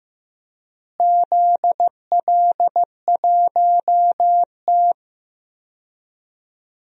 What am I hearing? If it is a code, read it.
Morse code, heard as a keyed tone, about 15 wpm, 700 Hz. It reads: ZL1T
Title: DXCC-100 Call Signs - 4 Characters - Repeated 2x - 15wpm